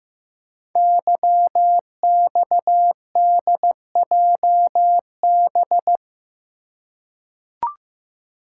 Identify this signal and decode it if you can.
Morse code, heard as a keyed tone, about 15 wpm, 700 Hz. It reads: YXDJB
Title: Code Groups - 5 Letters - 15wpm